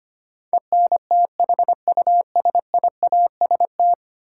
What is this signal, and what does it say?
Morse code, heard as a keyed tone, about 25 wpm, 700 Hz. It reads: ENTHUSIAST
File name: 10 Letter Words - First 3 Letters Given - 25wpm